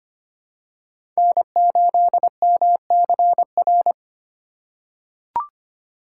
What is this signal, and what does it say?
Morse code, heard as a keyed tone, about 25 wpm, 700 Hz. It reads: N8MCR
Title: USA Call Signs - 5 Characters - 6x Character Spacing - 25wpm